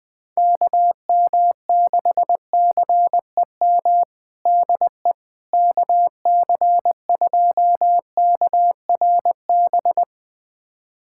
Morse code, heard as a keyed tone, about 21 wpm, 700 Hz. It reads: KM6CEM DE KC2KRB
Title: USA Callsign DE Callsign - 6 Characters - 20wpm